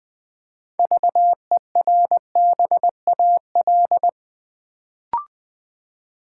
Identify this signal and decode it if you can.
Morse code, heard as a keyed tone, about 20 wpm, 700 Hz. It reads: VERBAL
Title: Build A Word - 3 Letters At A Time - 6-Letter Words - 20wpm